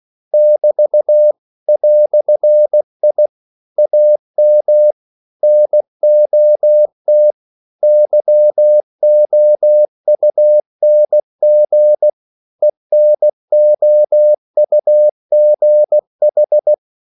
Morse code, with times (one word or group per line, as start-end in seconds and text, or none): 0.3-1.3 <BT>
1.7-3.3 "I
3.8-4.9 AM
5.4-7.3 NOT
7.8-12.1 YOUNG
12.6-16.7 ENOUGH